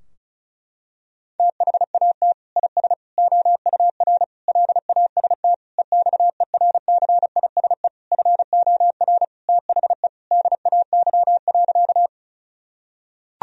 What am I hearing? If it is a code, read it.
Morse code, heard as a keyed tone, about 36 wpm, 700 Hz. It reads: THAT IS OUR LAST EXERCISE FOR THE DAY.
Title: Sentences from Top 1000 Words - Review Part 4 - 35wpm